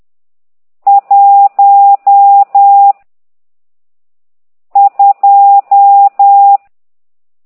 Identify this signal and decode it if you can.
Morse code, heard as a keyed tone, about 10 wpm, 800 Hz. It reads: 1 2
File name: ct2idw / exerc12